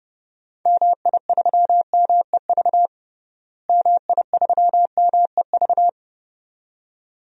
Morse code, heard as a keyed tone, about 30 wpm, 700 Hz. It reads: MI3MEV MI3MEV
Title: DXCC-100 Call Signs - 6 Characters - Repeated 2x - 30wpm